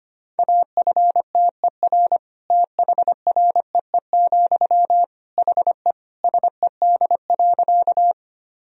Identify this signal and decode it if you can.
Morse code, heard as a keyed tone, about 25 wpm, 700 Hz. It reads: AFTER THREE, HE SED.
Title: Sentences from Top 200 Words and Abbreviations 25wpm